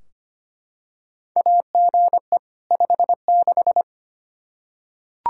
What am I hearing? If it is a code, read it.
Morse code, heard as a keyed tone, about 25 wpm, 700 Hz. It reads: AGE 56
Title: K1USN SST QSO Practice 25wpm